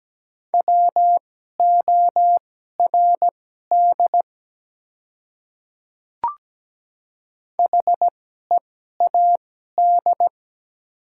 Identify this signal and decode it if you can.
Morse code, heard as a keyed tone, about 17 wpm, spaced as for 12 wpm, 700 Hz. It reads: WORD HEAD
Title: 2x Character Spacing - Top 300 Words - 17wpm